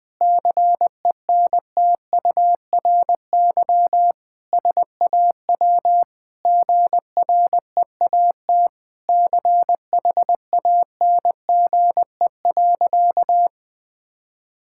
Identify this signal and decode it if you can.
Morse code, heard as a keyed tone, about 20 wpm, 700 Hz. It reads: CENTURY SAW GREAT CHANGE.
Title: Sentences with 5-Word Phrases - 20wpm